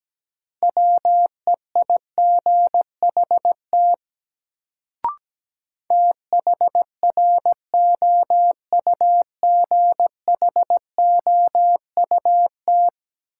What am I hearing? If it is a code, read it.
Morse code, heard as a keyed tone, about 17 wpm, 700 Hz. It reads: WEIGHT THROUGHOUT
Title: Sentences from Top 800 Words - New Word Drill - 17wpm